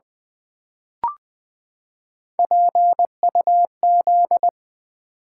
Morse code, heard as a keyed tone, about 20 wpm, 690 Hz. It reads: PUZ
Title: Build A Word - 3 Letters At A Time - 6-Letter Words - 20wpm